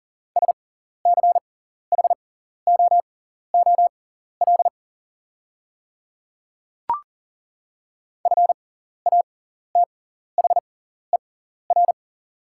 Morse code, heard as a keyed tone, about 40 wpm, spaced as for 14 wpm, 690 Hz. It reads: SCHOOL FATHER
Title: 6x Character Spacing - Top 300 Words - 40wpm